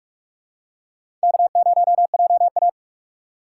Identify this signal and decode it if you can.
Morse code, heard as a keyed tone, about 44 wpm, 690 Hz. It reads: K0JA
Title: USA Call Signs - 4 Characters - 4x Character Spacing - 45wpm